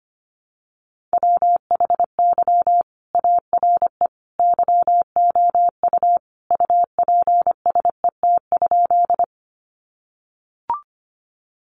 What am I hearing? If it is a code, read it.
Morse code, heard as a keyed tone, about 25 wpm, 700 Hz. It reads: WHY ARE YOU UPSET?
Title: Sentences from Top 800 Words 25wpm